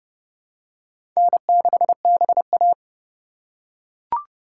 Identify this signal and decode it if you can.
Morse code, heard as a keyed tone, about 30 wpm, 700 Hz. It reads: N6BA
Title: USA Call Signs - 4x Character Spacing - 30wpm